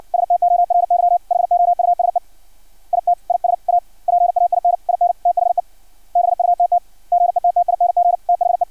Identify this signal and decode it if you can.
Morse code, heard as a keyed tone, about 56 wpm, 700 Hz. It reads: STORY HOUSE IT IS A PAIN IN THE BUTT GETTING THE
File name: archived / CRSnet